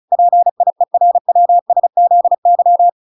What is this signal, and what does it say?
Morse code, heard as a keyed tone, about 34 wpm, 700 Hz. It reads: PIERWSZY